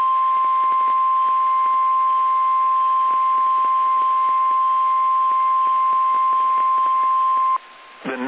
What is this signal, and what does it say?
Emergency Alert System (SAME) tones (the 1050 Hz attention tone), with speech. It is heard from 0.5 to 7.3 s.
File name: eas / KSDZKDJL / KSDZKDJL